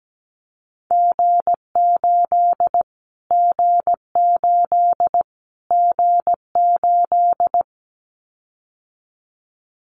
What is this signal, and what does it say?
Morse code, heard as a keyed tone, about 17 wpm, 700 Hz. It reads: G8 G8 G8
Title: Callsign Prefixes for Top 25 DXCC - Repeated 3x - 17wpm